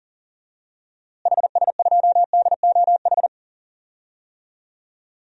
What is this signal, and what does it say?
Morse code, heard as a keyed tone, about 40 wpm, 700 Hz. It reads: HS2DOH
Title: DXCC-100 Call Signs - 6 Characters - Repeated 2x - 40wpm